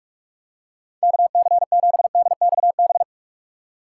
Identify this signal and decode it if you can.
Morse code, heard as a keyed tone, about 46 wpm, 700 Hz. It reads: KC7DXB
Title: USA Call Signs - 3x Character Spacing - 45wpm